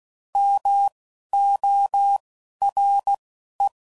Morse code, heard as a keyed tone, about 16 wpm, spaced as for 11 wpm, 800 Hz. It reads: MORE
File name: CW Code Groups @10WPM / Technical Subjects / XL